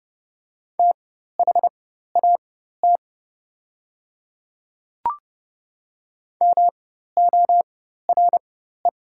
Morse code, heard as a keyed tone, about 32 wpm, spaced as for 14 wpm, 690 Hz. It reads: THAT MORE
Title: 4x Character Spacing - Top 100 Words - 30wpm